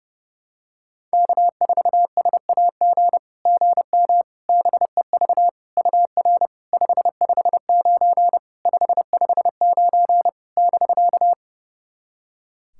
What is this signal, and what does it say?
Morse code, heard as a keyed tone, about 30 wpm, 700 Hz. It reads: K4SAG GM BEV UR 559 559 <BK>
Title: SOTA QSO Practice 30wpm